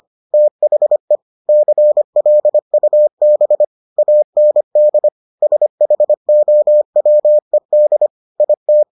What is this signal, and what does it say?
Morse code, heard as a keyed tone, about 25 wpm, 600 Hz. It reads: THE CLUB AND SHOWED IT